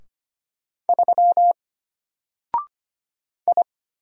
Morse code, heard as a keyed tone, about 25 wpm, 700 Hz. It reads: 3 I
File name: Single-Letter-Number-25wpm